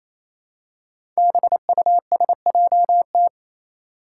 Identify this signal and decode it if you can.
Morse code, heard as a keyed tone, about 29 wpm, 700 Hz. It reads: BUSJT
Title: Five Challenging Letter-Number Combinations 28wpm